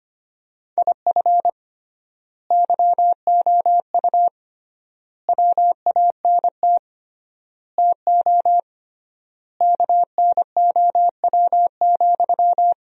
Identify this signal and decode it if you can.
Morse code, heard as a keyed tone, about 25 wpm, 700 Hz. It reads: IF YOU WANT TO KNOW,
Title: 3x Word-Spacing - Sentences from Top 200 Words - 25wpm